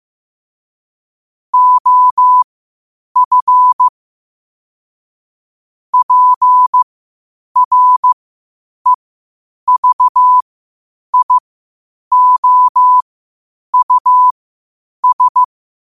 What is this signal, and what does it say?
Morse code, heard as a keyed tone, about 15 wpm, spaced as for 8 wpm, 1000 Hz. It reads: OF PREVIOUS
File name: qotdfiles / 8wpm